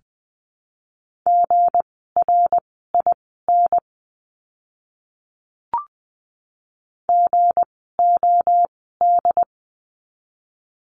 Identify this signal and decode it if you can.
Morse code, heard as a keyed tone, about 20 wpm, spaced as for 15 wpm, 700 Hz. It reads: GRIN GOD
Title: 160 Lesson 26f - Letter g - Words 20wpm